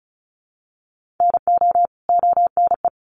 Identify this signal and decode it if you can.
Morse code, heard as a keyed tone, about 34 wpm, 700 Hz. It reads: NO ONE
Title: Sets of 2 Words 35wpm